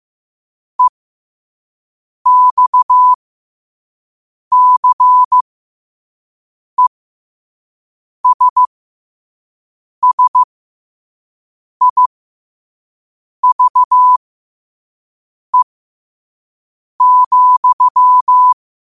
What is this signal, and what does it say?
Morse code, heard as a keyed tone, about 15 wpm, spaced as for 5 wpm, 1000 Hz. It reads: EXCESSIVE,
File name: qotdfiles / 5wpm